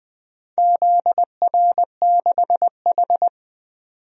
Morse code, heard as a keyed tone, about 20 wpm, 700 Hz. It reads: ZR6H
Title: DXCC-100 Call Signs - 4 Characters - 20wpm